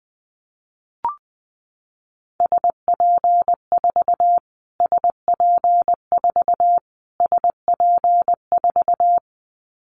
Morse code, heard as a keyed tone, about 20 wpm, 700 Hz. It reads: SP4 SP4 SP4
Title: Callsign Prefixes for Top 25 DXCC - Repeated 3x - 20wpm